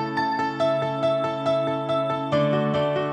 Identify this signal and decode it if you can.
There is no code.